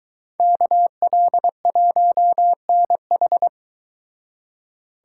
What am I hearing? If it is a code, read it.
Morse code, heard as a keyed tone, about 22 wpm, 700 Hz. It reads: KL1NH